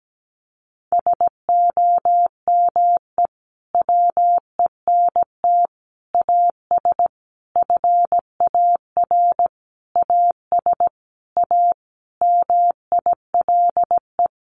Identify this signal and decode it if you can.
Morse code, heard as a keyed tone, about 17 wpm, 700 Hz. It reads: SOME WENT AS FAR AS A MILE
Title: Sentences from Top 1100 Words 17wpm